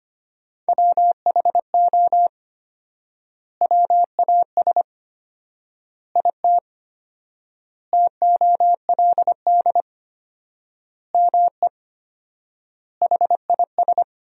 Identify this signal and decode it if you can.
Morse code, heard as a keyed tone, about 25 wpm, 700 Hz. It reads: WHO WAS IT TOLD ME HIS
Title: 4x Word-Spacing - Sentences from Top 400 Words - 25wpm